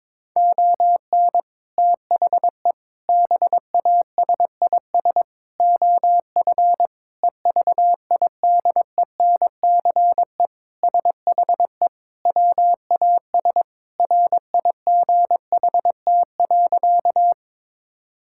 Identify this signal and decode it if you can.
Morse code, heard as a keyed tone, about 22 wpm, 700 Hz. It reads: ON THE BASIS OF EVIDENCE SHE WAS RIGHT.
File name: practice / 22wpm / Sentences with 4-Word Phrases - 22wpm